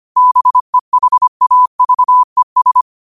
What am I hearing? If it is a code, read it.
Morse code, heard as a keyed tone, about 25 wpm, 1000 Hz. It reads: BEHAVES
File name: qotdfiles / 25wpm